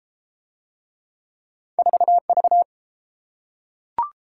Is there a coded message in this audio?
Yes: Morse code from 1.8 to 2.6 s, heard as a keyed tone, about 32 wpm, 700 Hz. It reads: 4V